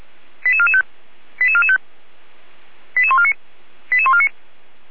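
Five-tone selective call (ZVEI), heard as quick runs of five tones, tones of about 70 ms. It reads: 88445 88445 88149 88149
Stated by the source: ZVEI five-tone call received by SDR.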